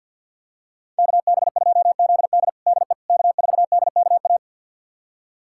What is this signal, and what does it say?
Morse code, heard as a keyed tone, about 47 wpm, 700 Hz. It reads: KB2ZD DE K4DKA
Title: USA Callsign DE Callsign - 5 Characters - 50wpm